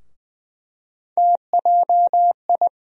Morse code, heard as a keyed tone, about 20 wpm, 700 Hz. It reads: TJI
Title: USA Call Sign Postfixes - 3 Letters 20wpm